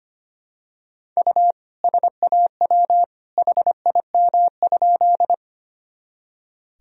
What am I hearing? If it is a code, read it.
Morse code, heard as a keyed tone, about 25 wpm, 700 Hz. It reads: U SAW HIM?
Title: Sentences from Top 300 Words and Abbreviations 25wpm